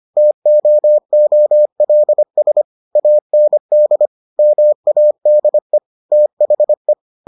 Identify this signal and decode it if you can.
Morse code, heard as a keyed tone, about 25 wpm, 600 Hz. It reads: TOOLS AND MADE THE